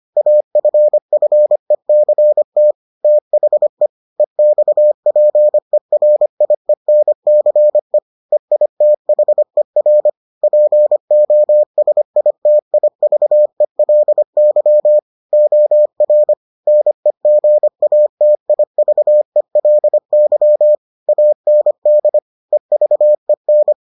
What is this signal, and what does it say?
Morse code, heard as a keyed tone, about 25 wpm, 600 Hz. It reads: AFFECT THE EXPERIENCE EITHER POSITIVELY OR NEGATIVELY AND EVEN